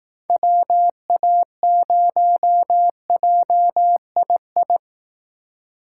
Morse code, heard as a keyed tone, about 18 wpm, 700 Hz. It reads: WA0JII
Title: Speed Racing - USA Call Signs 15wpm